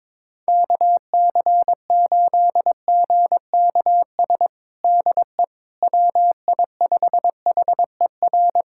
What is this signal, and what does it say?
Morse code, heard as a keyed tone, about 22 wpm, 700 Hz. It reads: KC8GKS DE WI5HER
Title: USA Callsign DE Callsign 22wpm